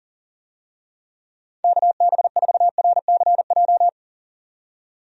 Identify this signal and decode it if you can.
Morse code, heard as a keyed tone, about 41 wpm, 700 Hz. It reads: KB4RCJ
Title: USA Call Signs - 6 Characters - 40wpm